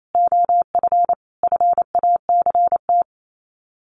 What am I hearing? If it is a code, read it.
Morse code, heard as a keyed tone, about 28 wpm, 700 Hz. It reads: OF FACT
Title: Sets of 4 Words 28wpm